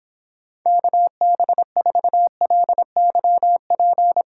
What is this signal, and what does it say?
Morse code, heard as a keyed tone, about 29 wpm, 700 Hz. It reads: KB4LYP